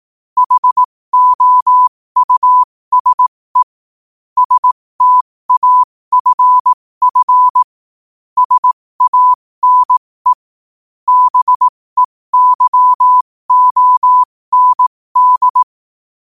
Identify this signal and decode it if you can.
Morse code, heard as a keyed tone, about 18 wpm, spaced as for 15 wpm, 1000 Hz. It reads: HOUSE STAFF SANE BEYOND
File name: qotdfiles / 15wpm